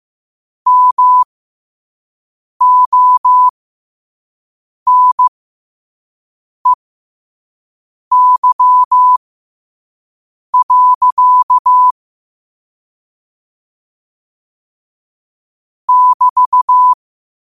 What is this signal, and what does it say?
Morse code, heard as a keyed tone, about 15 wpm, spaced as for 5 wpm, 1000 Hz. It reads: MONEY. <BT>
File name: qotdfiles / 5wpm